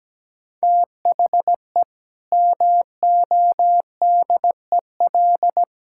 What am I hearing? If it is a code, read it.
Morse code, heard as a keyed tone, about 17 wpm, 700 Hz. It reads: THE MODEL
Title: Sentences from Top 1000 Words - Review Part 5 - 17wpm